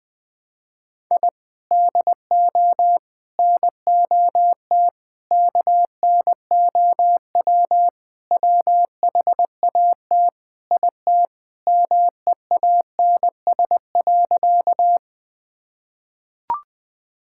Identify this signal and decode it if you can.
Morse code, heard as a keyed tone, about 21 wpm, 700 Hz. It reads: I DO NOT KNOW WHAT IT MEANS.
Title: Sentences with 5-Word Phrases - 20wpm